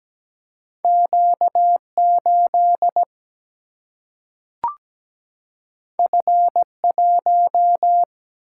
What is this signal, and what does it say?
Morse code, heard as a keyed tone, about 17 wpm, 700 Hz. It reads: Q8 F1
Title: Single-Letter Single-Number 17wpm